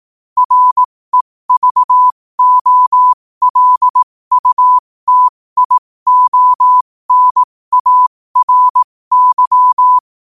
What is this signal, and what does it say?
Morse code, heard as a keyed tone, about 18 wpm, spaced as for 15 wpm, 1000 Hz. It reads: REVOLUTIONARY
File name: qotdfiles / 15wpm